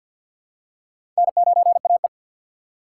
Morse code, heard as a keyed tone, about 53 wpm, 700 Hz. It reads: N9AE
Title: USA Call Signs - 4 Characters - 6x Character Spacing - 50wpm